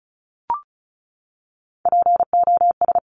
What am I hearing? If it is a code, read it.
Morse code, heard as a keyed tone, about 36 wpm, 700 Hz. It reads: POS
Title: Top Three Letter Combinations in Words 35wpm